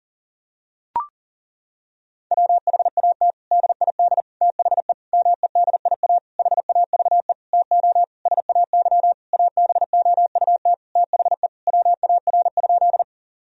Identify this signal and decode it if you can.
Morse code, heard as a keyed tone, about 37 wpm, 700 Hz. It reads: WHAT DID THE MEDIA HAVE TO SAY ABOUT THE WAR?
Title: Sentences from Top 1000 Words - Review Part 4 - 40wpm